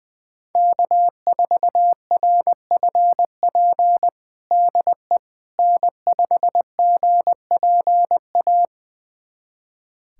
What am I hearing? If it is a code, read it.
Morse code, heard as a keyed tone, about 20 wpm, 700 Hz. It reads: K4RFP DE N5GPA
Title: USA Callsign DE Callsign - 5 Characters - 20wpm